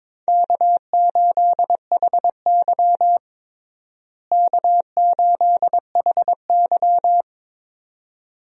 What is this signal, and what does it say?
Morse code, heard as a keyed tone, about 22 wpm, 700 Hz. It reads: K8HY K8HY
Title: USA Call Signs - 4 Characters - Repeated 2x - 22wpm